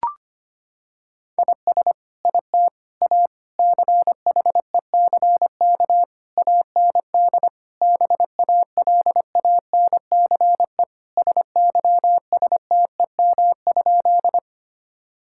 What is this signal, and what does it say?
Morse code, heard as a keyed tone, about 25 wpm, 700 Hz. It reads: IS IT A CHECK AND BALANCE SYSTEM?
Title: Sentences from Top 1000 Words - Review Part 6 - 25wpm